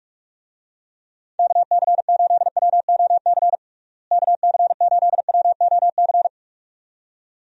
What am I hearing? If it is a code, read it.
Morse code, heard as a keyed tone, about 46 wpm, 700 Hz. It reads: KC8WOC KC8WOC
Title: USA Call Signs - 6 Characters - Repeated 2x - 45wpm